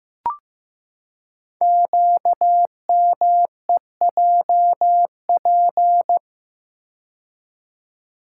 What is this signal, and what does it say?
Morse code, heard as a keyed tone, about 15 wpm, 700 Hz. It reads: QMEJP
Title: Code Groups - 5 Letter:Numbers - 15wpm